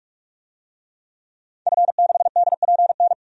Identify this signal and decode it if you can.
Morse code, heard as a keyed tone, about 45 wpm, 700 Hz. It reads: F6DPN
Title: DXCC-100 Call Signs - 5 Characters - 45wpm